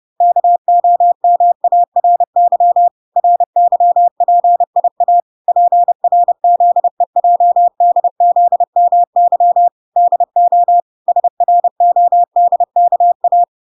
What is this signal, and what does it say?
Morse code, heard as a keyed tone, about 30 wpm, 700 Hz. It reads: KOMARY RYPIA PRZEJDZMY DO SRODKA